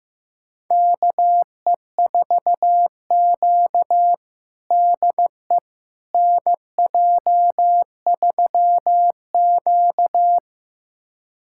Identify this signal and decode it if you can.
Morse code, heard as a keyed tone, about 15 wpm, 700 Hz. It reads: KE4Q DE NJ3Q
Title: USA Callsign DE Callsign - 4 Characters - 15wpm